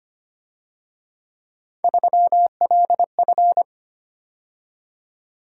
Morse code, heard as a keyed tone, about 30 wpm, 700 Hz. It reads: 3LF
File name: Speed-Racing - Three Letter-Number Combinations - 25wpm